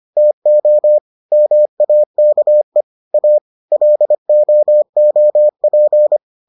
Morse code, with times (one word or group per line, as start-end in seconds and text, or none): 0.2-1.0 TO
1.3-2.8 MAKE
3.1-3.4 A
3.7-6.2 LOOP